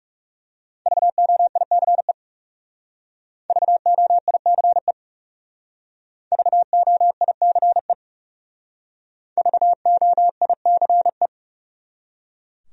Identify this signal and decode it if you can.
Morse code, heard as a keyed tone, about 37 wpm, 700 Hz. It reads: VOICE VOICE VOICE VOICE
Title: Speed-Racing Top 500 Words 30wpm